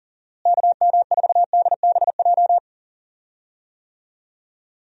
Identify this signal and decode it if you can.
Morse code, heard as a keyed tone, about 40 wpm, 700 Hz. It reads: KM4DBJ
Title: USA Call Signs - 6 Characters - Repeated 2x - 40wpm